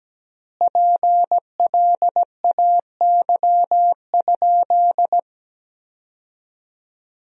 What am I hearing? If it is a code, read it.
Morse code, heard as a keyed tone, about 17 wpm, 700 Hz. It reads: PLAY?
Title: 4x Word-Spacing - Sentences from Top 200 Words - 17wpm